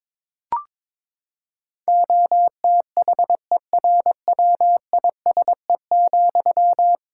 Morse code, heard as a keyed tone, about 22 wpm, 700 Hz. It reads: OTHERWISE,
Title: Sentences from Top 1600 Words 22wpm